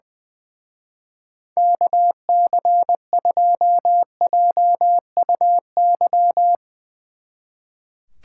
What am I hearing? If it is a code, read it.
Morse code, heard as a keyed tone, about 21 wpm, 700 Hz. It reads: KC2JUY
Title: Speed Racing - USA 2x3 Call Signs 20wpm